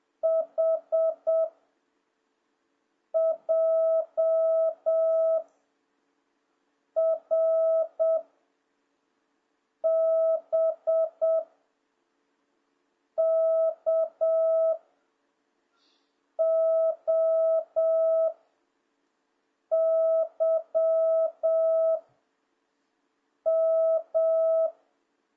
Morse code, heard as a keyed tone, about 7 wpm, 660 Hz. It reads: H J R B K O Y M